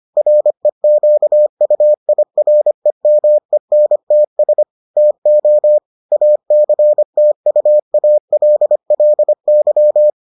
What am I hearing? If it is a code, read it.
Morse code, heard as a keyed tone, about 25 wpm, 600 Hz. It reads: REQUIREMENTS TO ACTUALLY